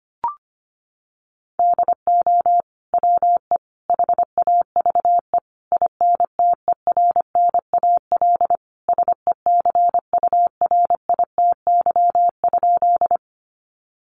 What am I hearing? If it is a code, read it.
Morse code, heard as a keyed tone, about 25 wpm, 700 Hz. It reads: DO WE HAVE INTERNAL SECURITY?
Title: Sentences from Top 1400 Words 25wpm